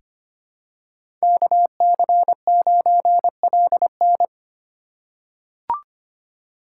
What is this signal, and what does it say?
Morse code, heard as a keyed tone, about 25 wpm, 700 Hz. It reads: KC9LN
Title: USA Call Signs - 3x Character Spacing - 25wpm